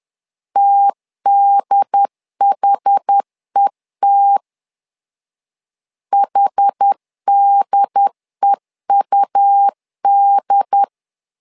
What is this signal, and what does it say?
Morse code, heard as a keyed tone, about 10 wpm, 800 Hz. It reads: TDHET HDEUD
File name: LEZIONI / 2 UHDN / 2 UHD 10wpm 1826